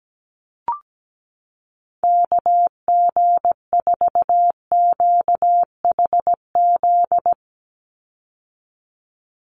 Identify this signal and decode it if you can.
Morse code, heard as a keyed tone, about 17 wpm, 700 Hz. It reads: KG4QHZ
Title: USA Call Signs 17wpm